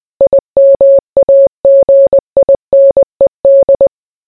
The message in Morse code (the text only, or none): IMAGINED